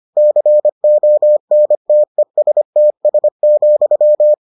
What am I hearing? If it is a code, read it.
Morse code, heard as a keyed tone, about 25 wpm, 600 Hz. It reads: CONTESTS,